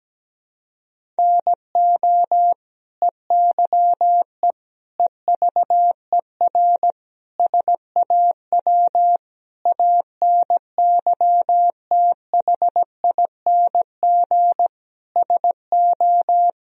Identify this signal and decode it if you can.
Morse code, heard as a keyed tone, about 17 wpm, 700 Hz. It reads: NO EYE EVER SAW ANYTHING SO
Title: Sentences from Top 1000 Words - Review Part 1 - 17wpm